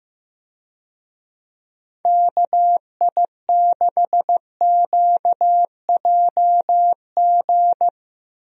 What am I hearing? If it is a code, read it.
Morse code, heard as a keyed tone, about 15 wpm, 700 Hz. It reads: KI6QJG
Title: USA Call Signs - 6 Characters - Repeated 2x - 15wpm